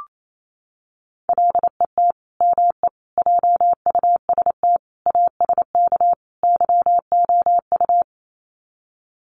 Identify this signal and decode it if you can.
Morse code, heard as a keyed tone, about 28 wpm, 700 Hz. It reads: LET ME JUST ASK YOU
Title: Sets of 5 Words Encore2 28wpm